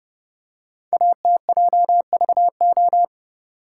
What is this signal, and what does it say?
Morse code, heard as a keyed tone, about 30 wpm, 700 Hz. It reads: ATJVO
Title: Code Groups - 5 Letters - 30wpm